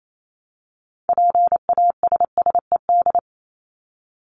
Morse code, heard as a keyed tone, about 28 wpm, 700 Hz. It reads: PASSED
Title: ED Suffix - Top 100 Words with Base Words 28wpm